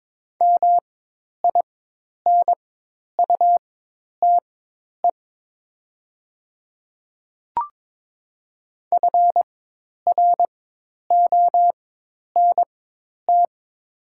Morse code, heard as a keyed tone, about 21 wpm, spaced as for 10 wpm, 700 Hz. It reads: MINUTE FRONT
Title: 4x Character Spacing - Top 500 Words - 22wpm